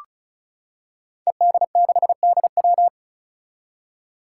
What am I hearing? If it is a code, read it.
Morse code, heard as a keyed tone, about 36 wpm, 700 Hz. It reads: ED6DW
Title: Five Challenging Letter-Number Combinations 35wpm